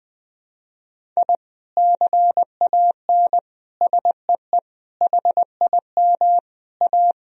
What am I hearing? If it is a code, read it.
Morse code, heard as a keyed tone, about 21 wpm, 700 Hz. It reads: I CAN SEE HIM A
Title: Sentences from Top 1000 Words - Review Part 1 - 20wpm